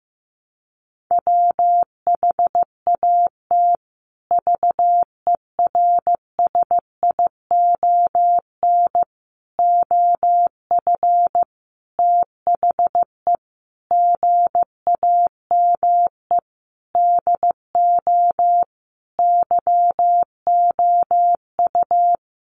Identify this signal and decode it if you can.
Morse code, heard as a keyed tone, about 15 wpm, 700 Hz. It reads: WHAT VERSION OF THE GAME DO YOU
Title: Sentences from Top 1400 Words 15wpm